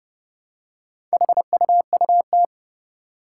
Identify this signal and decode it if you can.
Morse code, heard as a keyed tone, about 30 wpm, 700 Hz. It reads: HUUT
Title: Four Challenging Letter-Number Combinations 30wpm